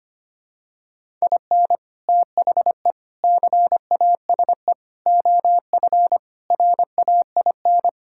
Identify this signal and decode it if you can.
Morse code, heard as a keyed tone, about 25 wpm, 700 Hz. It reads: IN THE CASE OF RAIN
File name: practice / 25wpm / Sentences with 4-Word Phrases - 25wpm